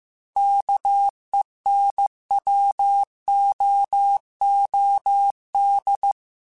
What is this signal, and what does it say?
Morse code, heard as a keyed tone, about 15 wpm, 800 Hz. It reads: KENWOOD
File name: QSOs / QSOs-mp3s / WA7QOX 2@13